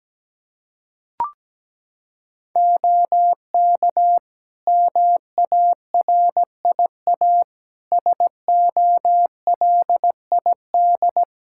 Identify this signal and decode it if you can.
Morse code, heard as a keyed tone, about 17 wpm, 700 Hz. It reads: OK MARIA SOLID
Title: Practice QSO - Exchange 3 - More Info - 17wpm